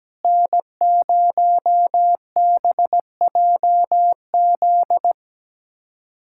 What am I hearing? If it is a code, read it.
Morse code, heard as a keyed tone, about 17 wpm, 700 Hz. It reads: N0BJZ
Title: USA Callsign DE Callsign - 5 Characters - 17wpm